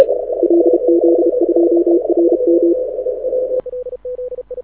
Morse code, heard as a keyed tone, about 30 wpm, 340 Hz, with radio noise. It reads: LZ2RM
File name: audio zawody / IARU 2015 CW SN7Q